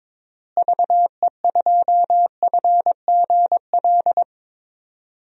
Morse code, heard as a keyed tone, about 22 wpm, 700 Hz. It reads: VE2FGL
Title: DXCC-100 Call Signs - 6 Characters - Repeated 3x - 22wpm